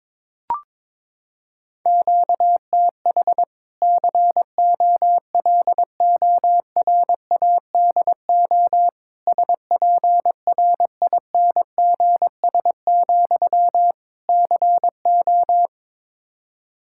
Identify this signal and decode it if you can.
Morse code, heard as a keyed tone, about 22 wpm, 700 Hz. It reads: QTH COLORADO SPRINGS, CO
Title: Practice QSO - Exchange 1 - Beginning - 22wpm